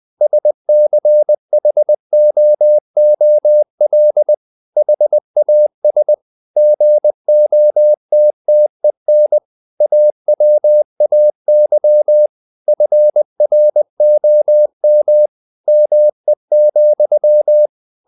Morse code, heard as a keyed tone, about 20 wpm, 600 Hz. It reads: SCHOOL HAS GOTTEN AWAY FROM ME,
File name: the old man and the sea / the old man and the sea part 1 20wpm